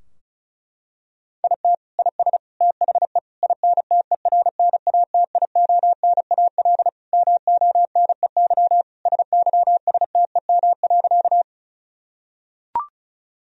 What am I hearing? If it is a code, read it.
Morse code, heard as a keyed tone, about 36 wpm, 700 Hz. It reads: IT IS THE INTERNATIONAL MONEY SYSTEM.
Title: Sentences from Top 1000 Words - Review Part 4 - 35wpm